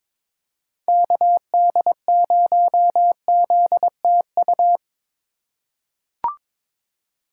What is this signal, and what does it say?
Morse code, heard as a keyed tone, about 22 wpm, 700 Hz. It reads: KD0ZTU
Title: USA Call Signs - 6 Characters - 4x Character Spacing - 22wpm